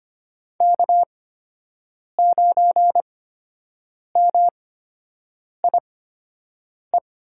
Morse code, heard as a keyed tone, about 25 wpm, spaced as for 7 wpm, 700 Hz. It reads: K9MIE
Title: USA Call Signs - 5 Characters - 8x Character Spacing - 25wpm